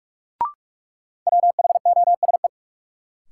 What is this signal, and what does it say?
Morse code, heard as a keyed tone, about 46 wpm, 700 Hz. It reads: WHOSE
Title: Sentences from Top 1200 Words - New Word Drill - 45wpm